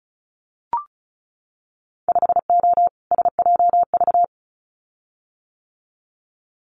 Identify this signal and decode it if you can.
Morse code, heard as a keyed tone, about 36 wpm, 700 Hz. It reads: 5O SJV
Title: Winter Field Day Exchanges Encore 35wpm